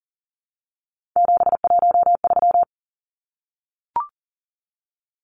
Morse code, heard as a keyed tone, about 40 wpm, 700 Hz. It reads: 713
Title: Numbers - 3-Digit 40wpm